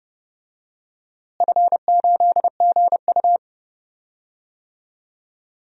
Morse code, heard as a keyed tone, about 30 wpm, 700 Hz. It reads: F8GU
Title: DXCC-100 Call Signs - 4 Characters - Repeated 2x - 30wpm